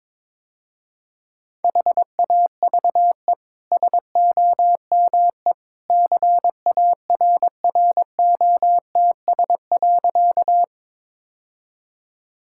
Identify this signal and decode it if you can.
Morse code, heard as a keyed tone, about 22 wpm, 700 Hz. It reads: HAVE SOME CARROTS.
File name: practice / 22wpm / Easy Sentences with Hints - 22wpm